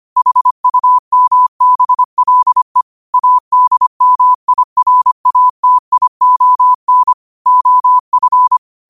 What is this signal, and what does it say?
Morse code, heard as a keyed tone, about 25 wpm, 1000 Hz. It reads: HUMBLE ADMIRATION OF
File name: qotdfiles / 25wpm